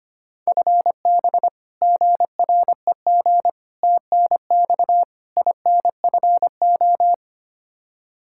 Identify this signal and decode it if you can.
Morse code, heard as a keyed tone, about 25 wpm, 700 Hz. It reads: FB GREG TNX INFO
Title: Practice QSO - Exchange 3 - More Info - 25wpm